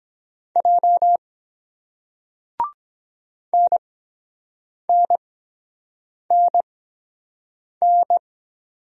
Morse code, heard as a keyed tone, about 23 wpm, spaced as for 7 wpm, 700 Hz. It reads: J NNNN